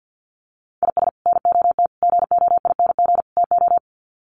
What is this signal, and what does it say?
Morse code, heard as a keyed tone, about 50 wpm, 700 Hz. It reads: IS NOT GOING TO
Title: Sets of 4 Words 50wpm